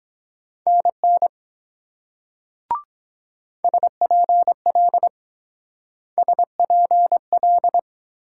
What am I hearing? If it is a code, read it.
Morse code, heard as a keyed tone, about 24 wpm, 700 Hz. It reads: NN SPL SPL